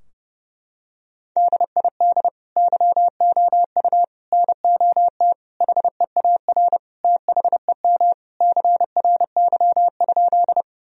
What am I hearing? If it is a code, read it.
Morse code, heard as a keyed tone, about 32 wpm, 700 Hz. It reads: DID YOU NOT HEAR THEM CRY?
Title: Sentences from Top 1000 Words - Review Part 2 - 30wpm